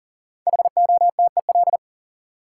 Morse code, heard as a keyed tone, about 40 wpm, 700 Hz. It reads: HOTEL
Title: Sentences from Top 1600 Words - New Word Drill Alt - 40wpm